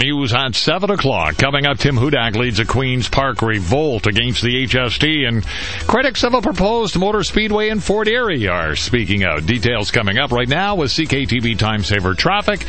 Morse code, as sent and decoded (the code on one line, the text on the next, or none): none